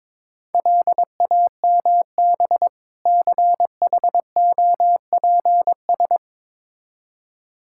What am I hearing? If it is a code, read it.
Morse code, heard as a keyed tone, about 22 wpm, 700 Hz. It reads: LAMB CHOPS
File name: Difficult Words with Hints - 22wpm